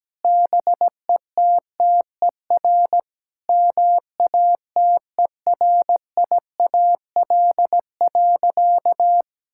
Morse code, heard as a keyed tone, about 17 wpm, 700 Hz. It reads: BETTER MATERIAL.
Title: Sentences from Top 1000 Words - Review Part 5 - 17wpm